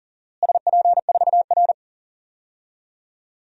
Morse code, heard as a keyed tone, about 37 wpm, 700 Hz. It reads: SP4R
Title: DXCC-100 Call Signs - 4 Characters - Repeated 2x - 40wpm